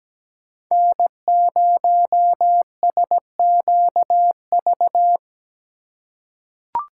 Morse code, heard as a keyed tone, about 17 wpm, 700 Hz. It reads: N0SQV
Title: USA Call Signs - 5 Characters - 3x Character Spacing - 17wpm